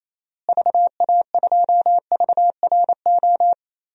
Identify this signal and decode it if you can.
Morse code, heard as a keyed tone, about 27 wpm, 700 Hz. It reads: VA2VRO
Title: DXCC-100 Call Signs - 6 Characters - Repeated 2x - 28wpm